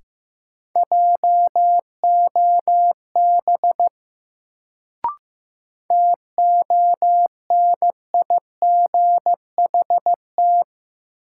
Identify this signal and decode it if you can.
Morse code, heard as a keyed tone, about 15 wpm, 700 Hz. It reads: JOB TONIGHT
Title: Sentences from Top 600 Words - New Word Drill - 15wpm